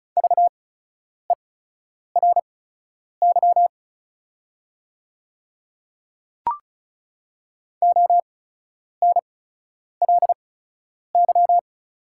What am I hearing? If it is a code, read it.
Morse code, heard as a keyed tone, about 34 wpm, spaced as for 10 wpm, 700 Hz. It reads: VERY ONLY
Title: 8x Character Spacing - Top 500 Words - 35wpm